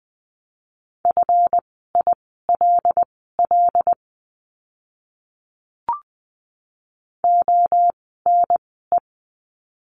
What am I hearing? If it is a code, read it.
Morse code, heard as a keyed tone, about 20 wpm, spaced as for 14 wpm, 700 Hz. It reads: FILL ONE
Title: 139 Lesson 23f - Letter f - Words 20wpm